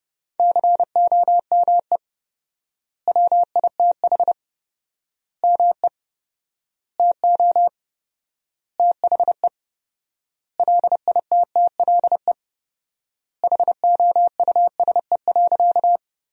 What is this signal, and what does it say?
Morse code, heard as a keyed tone, about 32 wpm, 700 Hz. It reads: COME WITH ME TO THE LITTLE HOUSE.
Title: 4x Word-Spacing - Sentences from Top 200 Words - 30wpm